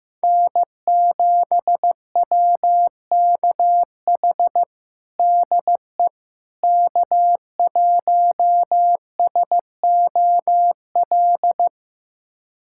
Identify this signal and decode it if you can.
Morse code, heard as a keyed tone, about 16 wpm, 700 Hz. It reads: N7WKH DE K1SOL
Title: USA Callsign DE Callsign - 5 Characters - 15wpm